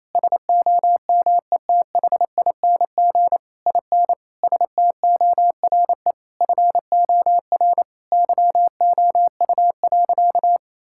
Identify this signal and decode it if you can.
Morse code, heard as a keyed tone, about 27 wpm, 700 Hz. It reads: SOMETHING IN STORE FOR YOU.
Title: Sentences from Top 1000 Words - Review Part 5 - 28wpm